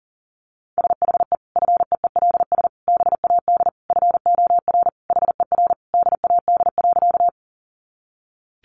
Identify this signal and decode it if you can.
Morse code, heard as a keyed tone, about 40 wpm, 700 Hz. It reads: SHE FEELS BAD FOR HER DAD.
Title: Additional Easy Phrases 40wpm